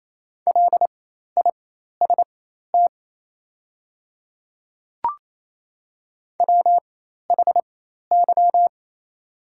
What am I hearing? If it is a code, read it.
Morse code, heard as a keyed tone, about 28 wpm, spaced as for 13 wpm, 700 Hz. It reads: LIST WHY
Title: 4x Character Spacing - Top 500 Words - 28wpm